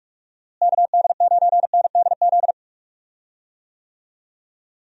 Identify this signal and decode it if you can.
Morse code, heard as a keyed tone, about 43 wpm, 700 Hz. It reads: KD9NDZ
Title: USA Call Signs - 6 Characters - Repeated 3x - 45wpm